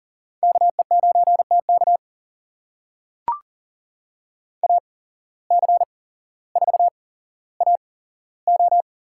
Morse code, heard as a keyed tone, about 40 wpm, spaced as for 17 wpm, 700 Hz. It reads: KE9TK AC4AO
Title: USA Call Signs - 5 Characters - 8x Character Spacing - 40wpm